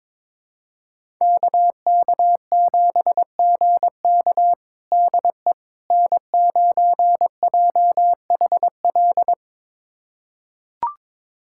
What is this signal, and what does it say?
Morse code, heard as a keyed tone, about 22 wpm, 700 Hz. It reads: KK7GK DE N9JHL
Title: USA Callsign DE Callsign - 5 Characters - 22wpm